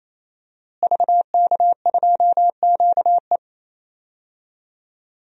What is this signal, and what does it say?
Morse code, heard as a keyed tone, about 28 wpm, 700 Hz. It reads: VK2QE
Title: DXCC-100 Call Signs - 5 Characters - Repeated 3x - 28wpm